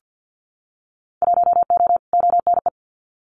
Morse code, heard as a keyed tone, about 50 wpm, 700 Hz. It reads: 1O ONE
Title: Winter Field Day Exchanges 50wpm